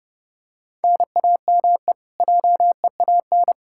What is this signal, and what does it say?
Morse code, heard as a keyed tone, about 30 wpm, 700 Hz. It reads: NAME JEAN
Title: Practice QSO - Exchange 1 - Beginning - 30wpm